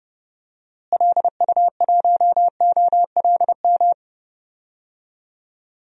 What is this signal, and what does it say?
Morse code, heard as a keyed tone, about 30 wpm, 700 Hz. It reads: LU1OLM
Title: DXCC-100 Call Signs - 6 Characters - Repeated 3x - 30wpm